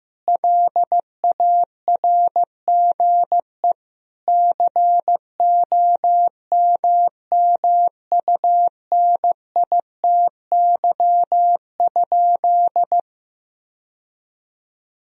Morse code, heard as a keyed tone, about 15 wpm, 700 Hz. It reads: LARGE COMMUNITY?
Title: Sentences from Top 1000 Words - Review Part 4 - 15wpm